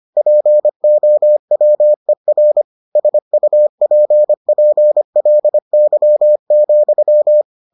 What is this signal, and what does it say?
Morse code, heard as a keyed tone, about 25 wpm, 600 Hz. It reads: POWER SUPPLY,